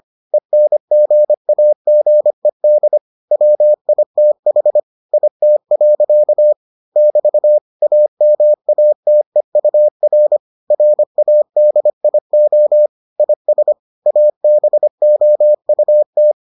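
Morse code, heard as a keyed tone, about 25 wpm, 600 Hz. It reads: ENGAGED WITH IT. <BT> AMATEUR RADIO IS ABOUT
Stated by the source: Morse edition of the Foundations of Amateur Radio episodes.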